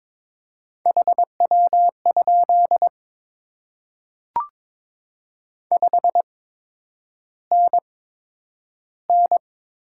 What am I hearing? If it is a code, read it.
Morse code, heard as a keyed tone, about 22 wpm, spaced as for 9 wpm, 700 Hz. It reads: HW? 5NN
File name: Top 20 QSO Elements - 8x Character Spacing - 22wpm